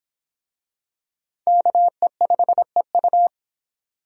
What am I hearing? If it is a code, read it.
Morse code, heard as a keyed tone, about 26 wpm, 700 Hz. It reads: KE5EU